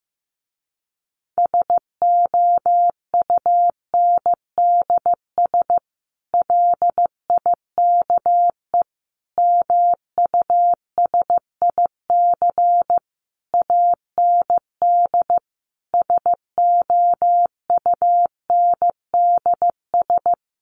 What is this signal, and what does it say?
Morse code, heard as a keyed tone, about 15 wpm, 700 Hz. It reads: SOUNDS LIKE MUSIC AND SOUNDS
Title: A Sound Of Thunder 15wpm